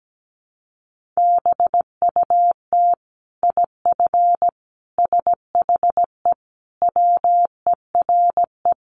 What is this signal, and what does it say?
Morse code, heard as a keyed tone, about 17 wpm, 700 Hz. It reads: BUT IF SHE WERE
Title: Sentences from Top 1500 Words 17wpm